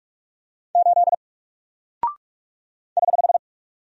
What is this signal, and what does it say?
Morse code, heard as a keyed tone, about 43 wpm, 700 Hz. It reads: 8 <HH>
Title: Single Character (Letter:Number:Punctuation:Prosign) Alt 45wpm